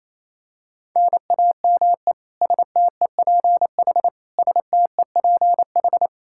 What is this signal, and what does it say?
Morse code, heard as a keyed tone, about 28 wpm, 700 Hz. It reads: NAME STEPH STEPH
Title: Practice QSO - Exchange 1 - Beginning - 28wpm